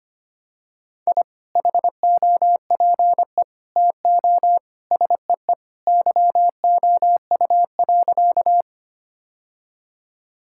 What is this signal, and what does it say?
Morse code, heard as a keyed tone, about 25 wpm, 700 Hz. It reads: I HOPE TO SEE YOU.
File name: practice / 25wpm / Sentences with 2-Word Phrases - 25wpm